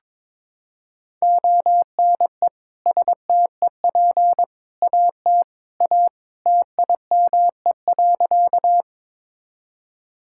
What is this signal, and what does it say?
Morse code, heard as a keyed tone, about 22 wpm, 700 Hz. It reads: ONE STEP AT A TIME.
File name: practice / 22wpm / Sentences with 3-Word Phrases - 22wpm